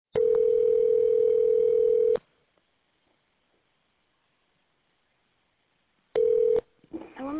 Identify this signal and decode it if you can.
Ringback tone (North America), 440 + 480 Hz.